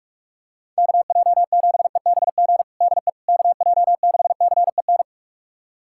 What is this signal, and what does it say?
Morse code, heard as a keyed tone, about 43 wpm, 700 Hz. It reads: KJ7EBG DE KJ6CEN
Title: USA Callsign DE Callsign - 6 Characters - 45wpm